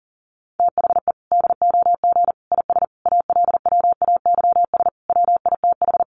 Morse code, heard as a keyed tone, about 40 wpm, 700 Hz. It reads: THE DOG IS ALWAYS WITH
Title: Sentences from Top 400 Words 40wpm